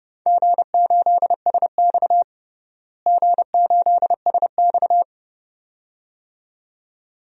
Morse code, heard as a keyed tone, about 30 wpm, 700 Hz. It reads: G8SX G8SX
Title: DXCC-100 Call Signs - 4 Characters - Repeated 2x - 30wpm